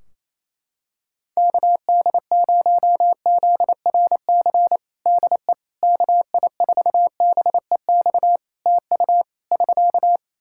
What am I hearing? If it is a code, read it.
Morse code, heard as a keyed tone, about 28 wpm, 700 Hz. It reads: KD0ZRC DE KI4BEX TU <SK>
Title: Practice QSO - Exchange 4 - Ending - 28wpm